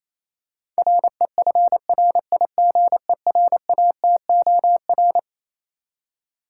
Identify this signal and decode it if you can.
Morse code, heard as a keyed tone, about 27 wpm, 700 Hz. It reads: REFRIGERATOR
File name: Easy Words - Speed-Racing with Hints - 22wpm